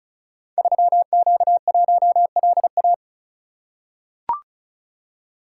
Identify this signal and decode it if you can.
Morse code, heard as a keyed tone, about 36 wpm, 700 Hz. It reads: 3Q1LA
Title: Code Groups - 5 Letter:Numbers - 35wpm